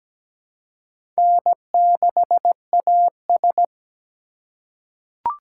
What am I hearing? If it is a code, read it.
Morse code, heard as a keyed tone, about 17 wpm, 700 Hz. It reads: N6AS
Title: USA Call Signs - 4 Characters - 3x Character Spacing - 17wpm